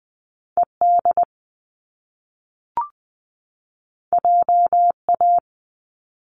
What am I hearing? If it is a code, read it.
Morse code, heard as a keyed tone, about 20 wpm, 700 Hz. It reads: E D J A
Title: Top Two Letter Combinations in Words 20wpm